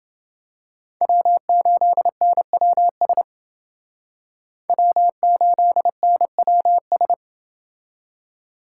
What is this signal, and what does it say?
Morse code, heard as a keyed tone, about 29 wpm, 700 Hz. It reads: W8NWS W8NWS